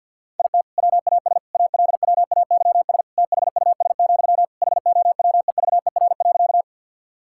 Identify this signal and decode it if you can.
Morse code, heard as a keyed tone, about 50 wpm, 700 Hz. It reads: IT WAS ALWAYS THUS, HOWEVER.
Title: Sentences from Top 1000 Words - Review Part 5 - 50wpm